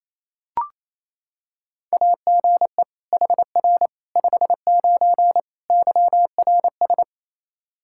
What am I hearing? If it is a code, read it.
Morse code, heard as a keyed tone, about 27 wpm, 700 Hz. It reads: AGE HR 59 YRS
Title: K1USN SST QSO Practice 28wpm